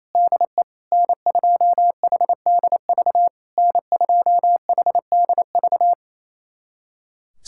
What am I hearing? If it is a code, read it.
Morse code, heard as a keyed tone, about 28 wpm, 700 Hz. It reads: DE N2HDV N2HDV
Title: SOTA QSO Practice 28wpm